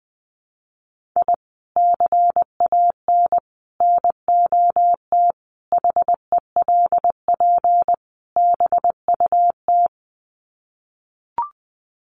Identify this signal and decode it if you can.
Morse code, heard as a keyed tone, about 20 wpm, 700 Hz. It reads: I CAN NOT HELP BUT
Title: Sets of 5 Words Encore2 20wpm